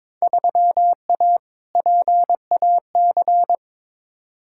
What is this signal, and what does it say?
Morse code, heard as a keyed tone, about 22 wpm, 700 Hz. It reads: 3A PAC
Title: Field Day Exchanges Encore - 22wpm